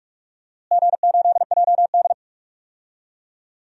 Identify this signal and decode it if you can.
Morse code, heard as a keyed tone, about 46 wpm, 700 Hz. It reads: G8JD
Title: DXCC-100 Call Signs - 4 Characters - Repeated 2x - 45wpm